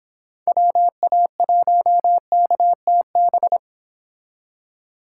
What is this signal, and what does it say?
Morse code, heard as a keyed tone, about 30 wpm, 700 Hz. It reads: WA1KTB